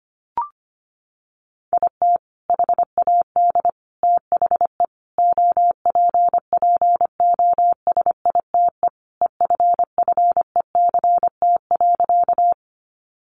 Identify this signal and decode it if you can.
Morse code, heard as a keyed tone, about 25 wpm, 700 Hz. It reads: IT HAD THE OPPOSITE EFFECT.
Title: Sentences from Top 1300 Words 25wpm